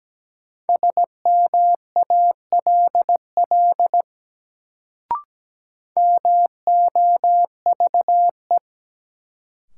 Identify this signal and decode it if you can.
Morse code, heard as a keyed tone, about 17 wpm, 700 Hz. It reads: SMALL MOVE
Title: Sentences from Top 200 Words - New Word Drill - 17wpm